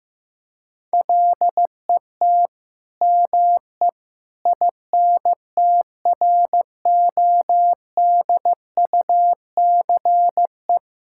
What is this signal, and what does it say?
Morse code, heard as a keyed tone, about 15 wpm, 700 Hz. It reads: LET ME INTRODUCE
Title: Sentences from Top 1600 Words 15wpm